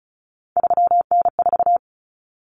Morse code, heard as a keyed tone, about 34 wpm, 700 Hz. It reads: 3N4
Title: Common Cut Numbers - 3-Digit 35wpm